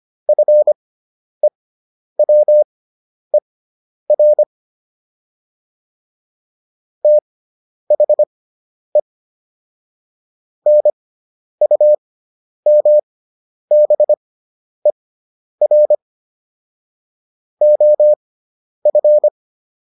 Morse code, heard as a keyed tone, about 25 wpm, spaced as for 10 wpm, 600 Hz. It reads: FEWER THE NUMBER OF